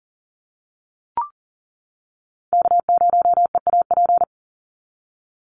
Morse code, heard as a keyed tone, about 40 wpm, 700 Hz. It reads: K0EAP
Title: NA SOTA Chaser Callsigns 40wpm